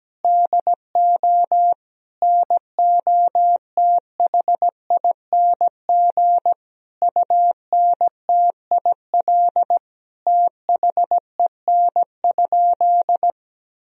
Morse code, heard as a keyed tone, about 17 wpm, 700 Hz. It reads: DO NOTHING UNTIL THEN?
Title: Sentences from Top 500 Words and Abbreviations 17wpm